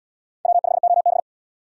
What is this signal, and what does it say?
Morse code, heard as a keyed tone, about 73 wpm, 700 Hz. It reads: W5OB